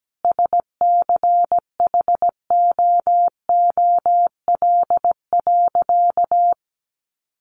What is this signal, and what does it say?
Morse code, heard as a keyed tone, about 17 wpm, 700 Hz. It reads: SCHOOL.
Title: Additional Easy Phrases 17wpm